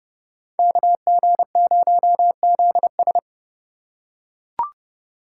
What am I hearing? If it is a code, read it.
Morse code, heard as a keyed tone, about 31 wpm, 700 Hz. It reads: KG0ZS